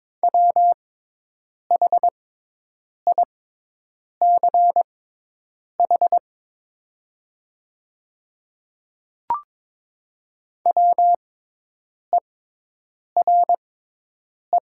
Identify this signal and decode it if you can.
Morse code, heard as a keyed tone, about 22 wpm, spaced as for 8 wpm, 700 Hz. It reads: WHICH WERE
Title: 6x Character Spacing - Top 200 Words - 22wpm